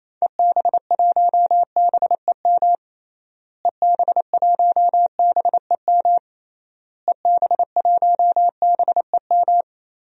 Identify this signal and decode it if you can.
Morse code, heard as a keyed tone, about 28 wpm, 700 Hz. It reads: EB1BEM EB1BEM EB1BEM
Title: DXCC-100 Call Signs - 6 Characters - Repeated 3x - 28wpm